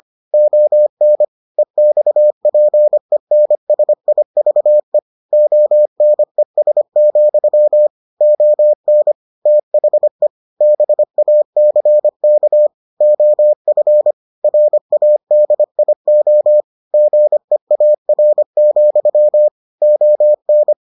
Morse code, heard as a keyed tone, about 25 wpm, 600 Hz. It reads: ON EXPENSIVE ONES, ON THE BACK OF RADIO GEAR, ON